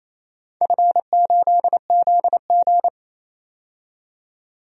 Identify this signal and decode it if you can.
Morse code, heard as a keyed tone, about 28 wpm, 700 Hz. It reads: F8ZG
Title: DXCC-100 Call Signs - 4 Characters - 28wpm